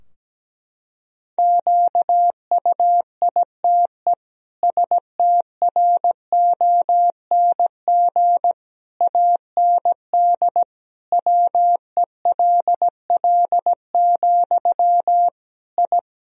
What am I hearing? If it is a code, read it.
Morse code, heard as a keyed tone, about 17 wpm, 700 Hz. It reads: QUITE STRONG AND WELL, I
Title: Sentences from Top 1000 Words - Review Part 3 - 17wpm